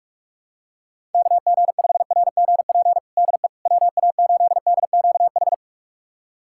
Morse code, heard as a keyed tone, about 41 wpm, 690 Hz. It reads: KG5RGP DE WA8DQH
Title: USA Callsign DE Callsign 45wpm